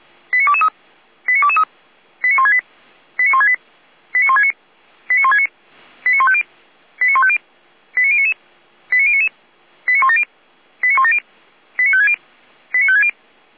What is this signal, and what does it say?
Five-tone selective call (ZVEI), heard as quick runs of five tones, tones of about 70 ms. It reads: 89223 89223 89168 89168 89169 89169 89150 89150 89099 89099 89170 89170 89560 89560